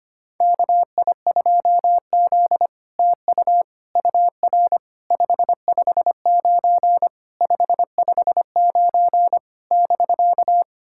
Morse code, heard as a keyed tone, about 25 wpm, 700 Hz. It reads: KI2Z TU UR 559 559 <BK>
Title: SOTA QSO Practice 25wpm